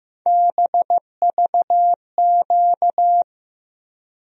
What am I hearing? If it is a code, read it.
Morse code, heard as a keyed tone, about 15 wpm, 700 Hz. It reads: BVQ
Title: USA Call Sign Postfixes - 3 Letters 15wpm